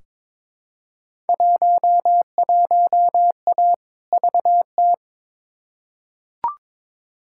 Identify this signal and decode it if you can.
Morse code, heard as a keyed tone, about 22 wpm, 700 Hz. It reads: 11A VT
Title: Field Day Exchanges Encore - 22wpm